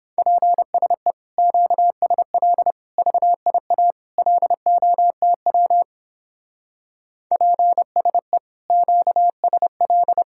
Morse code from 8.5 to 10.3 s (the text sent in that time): QSL